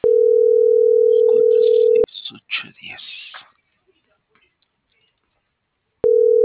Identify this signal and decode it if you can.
Ringback tone (North America), 440 + 480 Hz.